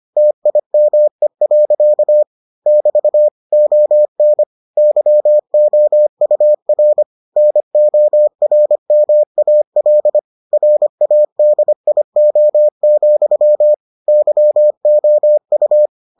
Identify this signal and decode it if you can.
Morse code, heard as a keyed tone, about 25 wpm, 600 Hz. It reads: TIME. <BT> ON YOUR NORMAL RADIO, YOU